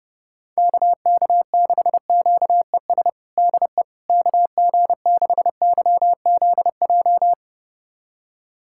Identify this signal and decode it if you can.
Morse code, heard as a keyed tone, about 32 wpm, 700 Hz. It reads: KK6QES DE KG6YZJ
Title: USA Callsign DE Callsign - 6 Characters - 30wpm